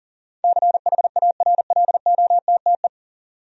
Morse code, heard as a keyed tone, about 37 wpm, 700 Hz. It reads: CHARLOTTE
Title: USA Common Names - 40wpm